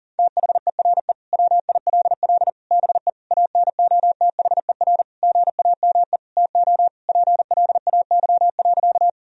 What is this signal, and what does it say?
Morse code, heard as a keyed tone, about 40 wpm, 700 Hz. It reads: THERE WILL BE ANOTHER GAME TO PLAY.
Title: Sentences from Top 1000 Words - Review Part 3 - 40wpm